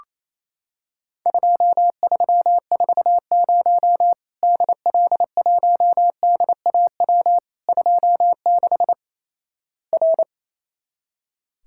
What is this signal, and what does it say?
Morse code, heard as a keyed tone, about 28 wpm, 700 Hz. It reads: 2340 DL1DAW 26 R
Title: Worked All Europe DX Contest - QTC Exchange - 28wpm